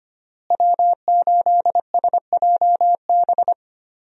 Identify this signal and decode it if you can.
Morse code, heard as a keyed tone, about 25 wpm, 700 Hz. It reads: W8SJB
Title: USA Call Signs - 5 Characters - Repeated 3x - 25wpm